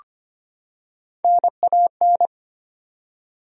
Morse code, heard as a keyed tone, about 25 wpm, 700 Hz. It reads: NAN
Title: USA Common Names - Max 3 Letters - 25wpm